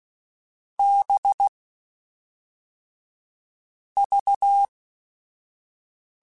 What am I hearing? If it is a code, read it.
Morse code, heard as a keyed tone, about 16 wpm, 800 Hz. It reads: B V
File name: Groups@5 / Groups@5 mp3s / VB46@5